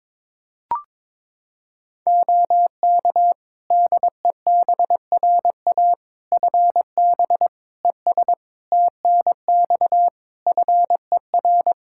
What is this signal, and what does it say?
Morse code, heard as a keyed tone, about 22 wpm, 700 Hz. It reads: OK DEBRA FB ES TNX FER
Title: Practice QSO - Exchange 2 - Info - 22wpm